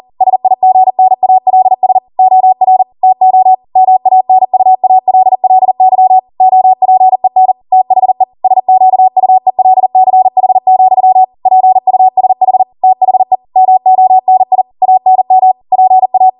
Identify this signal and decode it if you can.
Morse code, heard as a keyed tone, about 40 wpm, 750 Hz. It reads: SIGNALS OR TO MANUALLY OPEN THE SQUELCH, PUSH THE MONI ANM PA
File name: Morse / 40WPM